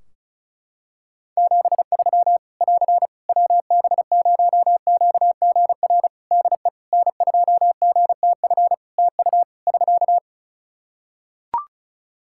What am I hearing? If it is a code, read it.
Morse code, heard as a keyed tone, about 34 wpm, 700 Hz. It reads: 73 <AR> WB0QGR DE N2GTF TU <SK>
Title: Practice QSO - Exchange 4 - Ending - 35wpm